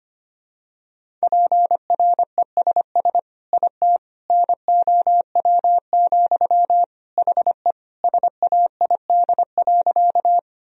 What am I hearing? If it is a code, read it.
Morse code, heard as a keyed tone, about 25 wpm, 700 Hz. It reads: PRESS IT NOW, HE SAID.
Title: Sentences from Top 1000 Words - Review Part 1 - 25wpm